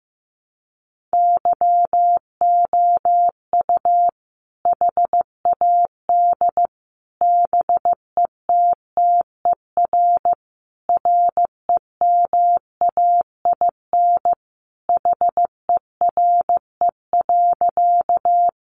Morse code, heard as a keyed tone, about 15 wpm, 700 Hz. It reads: YOU HAD BETTER REMAIN HERE.
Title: Sentences from Top 1200 Words 15wpm